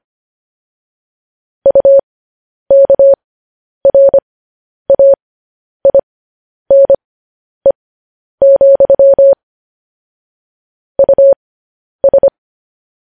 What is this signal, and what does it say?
Morse code, heard as a keyed tone, about 25 wpm, spaced as for 10 wpm, 550 Hz. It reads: UKRAINE, US
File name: podcast / News headlines-1x-25x10